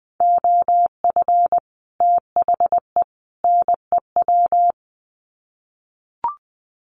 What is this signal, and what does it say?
Morse code, heard as a keyed tone, about 20 wpm, 700 Hz. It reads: OF THE NEW
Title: Sets of 3 Words 20wpm